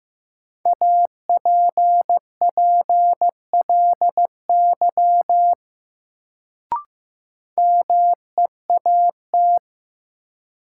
Morse code, heard as a keyed tone, about 15 wpm, 700 Hz. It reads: APPLY MEAT
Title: Sentences from Top 1200 Words - New Word Drill Alt - 15wpm